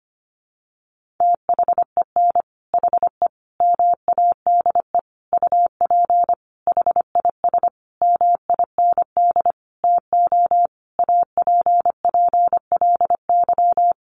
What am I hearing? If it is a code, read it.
Morse code, heard as a keyed tone, about 25 wpm, 700 Hz. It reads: THEN HE MADE UP HIS MIND TO APPLY
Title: Sentences from Top 1400 Words 25wpm